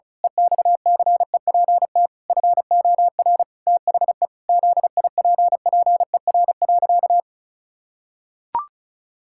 Morse code, heard as a keyed tone, about 36 wpm, 700 Hz. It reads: EXCEPT FOR THE ZIPPER.
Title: Pangram Challenge v2 - 35wpm